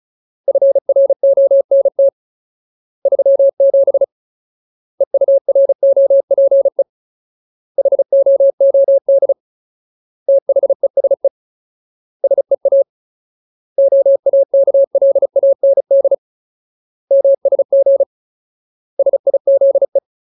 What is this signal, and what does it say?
Morse code, heard as a keyed tone, about 34 wpm, 550 Hz. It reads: FRONT 37 EUROPE HOOD THESE SEA OAKLAND MSG SIZE
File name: Ultimate Word List / 35WPM Ultimate